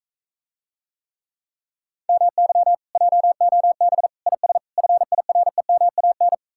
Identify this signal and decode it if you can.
Morse code, heard as a keyed tone, about 45 wpm, 700 Hz. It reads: MY JOB IS FIREMAN